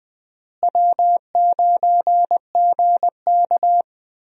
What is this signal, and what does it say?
Morse code, heard as a keyed tone, about 20 wpm, 700 Hz. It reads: W9GK
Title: USA Call Signs - 4 Characters - Repeated 3x - 20wpm